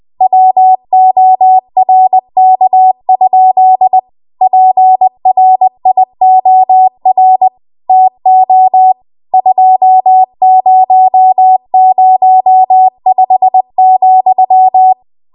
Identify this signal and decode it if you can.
Morse code, heard as a keyed tone, about 20 wpm, 750 Hz. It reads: WORK? PRIOR TO 2005,
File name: Morse / 20WPM